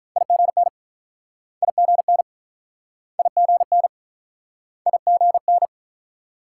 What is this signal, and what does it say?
Morse code, heard as a keyed tone, about 45 wpm, 700 Hz. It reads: IGN IGN IGN IGN